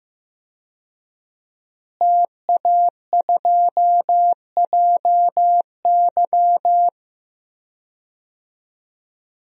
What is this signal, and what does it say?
Morse code, heard as a keyed tone, about 15 wpm, 700 Hz. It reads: TA2JY
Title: DXCC-100 Call Signs - 5 Characters - Repeated 2x - 15wpm